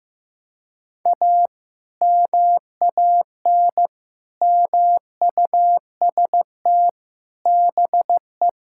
Morse code, heard as a keyed tone, about 15 wpm, 700 Hz. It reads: A MAN MUST BE
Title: Sentences from Top 1000 Words - Review Part 6 - 15wpm